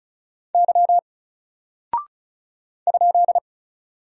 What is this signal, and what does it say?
Morse code, heard as a keyed tone, about 36 wpm, 700 Hz. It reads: Y ?
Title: Single Character (Letter:Number:Punctuation:Prosign) Alt 35wpm